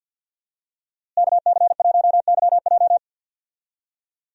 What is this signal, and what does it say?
Morse code, heard as a keyed tone, about 51 wpm, 700 Hz. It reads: KC1YJ
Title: USA Call Signs - 5 Characters - Repeated 3x - 50wpm